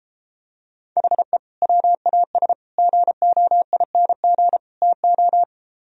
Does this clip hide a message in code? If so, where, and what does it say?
Yes: Morse code from 1.0 to 5.4 s, heard as a keyed tone, about 33 wpm, 700 Hz. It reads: HE WAS GOING TO